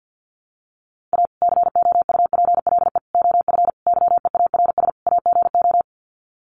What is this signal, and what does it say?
Morse code, heard as a keyed tone, about 50 wpm, 700 Hz. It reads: A COUPLE OF YEARS AGO
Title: Sets of 5 Words 50wpm